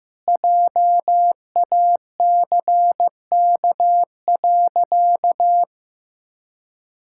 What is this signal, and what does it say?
Morse code, heard as a keyed tone, about 15 wpm, 700 Hz. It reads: JACK.
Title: Pangram Challenge v2 - 15wpm